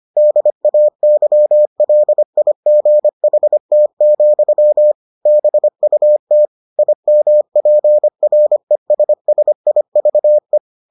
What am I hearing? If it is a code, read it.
Morse code, heard as a keyed tone, about 25 wpm, 600 Hz. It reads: DAYLIGHT, BUT IMPRESSIVE